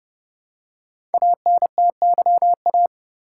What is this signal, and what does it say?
Morse code, heard as a keyed tone, about 30 wpm, 700 Hz. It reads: ANTYA
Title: Code Groups - 5 Letter:Numbers - 30wpm